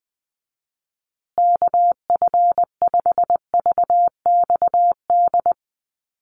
Morse code, heard as a keyed tone, about 19 wpm, 700 Hz. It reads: KF5VXD
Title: USA Call Signs Repeated 3x 20wpm